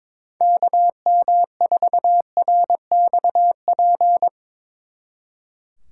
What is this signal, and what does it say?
Morse code, heard as a keyed tone, about 22 wpm, 700 Hz. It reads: KM4RXP
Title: Speed Racing - USA 2x3 Call Signs 22wpm